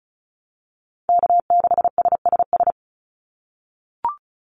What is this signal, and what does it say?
Morse code, heard as a keyed tone, about 34 wpm, 700 Hz. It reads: K6SSS
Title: Call Signs - 35wpm